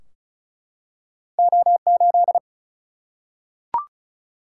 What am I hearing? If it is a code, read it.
Morse code, heard as a keyed tone, about 36 wpm, 700 Hz. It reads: O 8
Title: Single-Letter Single-Number 35wpm